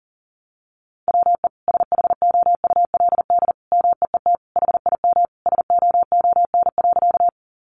Morse code, heard as a keyed tone, about 37 wpm, 700 Hz. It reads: WE SHOULD MEET HIM SOON.
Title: Sentences from Top 1100 Words 40wpm